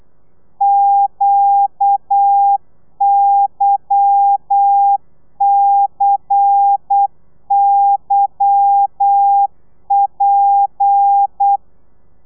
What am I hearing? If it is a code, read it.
Morse code, heard as a keyed tone, about 8 wpm, 800 Hz. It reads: QYCYP
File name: telegrafia / i2tfj / Lez10 35